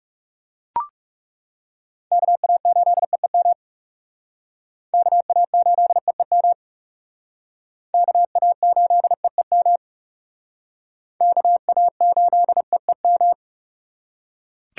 Morse code, heard as a keyed tone, about 37 wpm, 700 Hz. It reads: KA8EEM KA8EEM KA8EEM KA8EEM
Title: Speed Racing - USA 2x3 Call Signs 30wpm